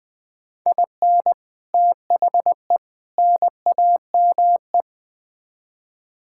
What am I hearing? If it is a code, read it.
Morse code, heard as a keyed tone, about 21 wpm, 700 Hz. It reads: IN THE NAME
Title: Build A Phrase - 4-Word Phrases - 20wpm